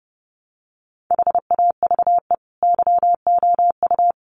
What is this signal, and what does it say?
Morse code, heard as a keyed tone, about 30 wpm, 700 Hz. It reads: HAVE YOU
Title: Sentences from Top 1100 Words 30wpm